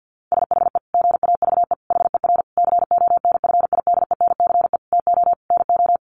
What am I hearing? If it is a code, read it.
Morse code, heard as a keyed tone, about 50 wpm, 700 Hz. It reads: SHE GAVE HER CONFIDENCE TO NO
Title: Sentences from Top 1500 Words 50wpm